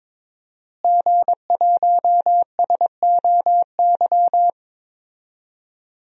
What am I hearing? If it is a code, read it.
Morse code, heard as a keyed tone, about 22 wpm, 700 Hz. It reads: G1SOY
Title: DXCC-100 Call Signs - 5 Characters - Repeated 2x - 22wpm